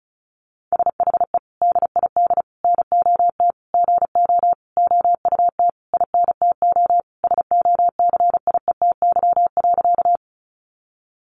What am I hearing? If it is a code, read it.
Morse code, heard as a keyed tone, about 34 wpm, 700 Hz. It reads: SHE DID NOT GO OUT INTO SOCIETY.
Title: Sentences from Top 700 Words 35wpm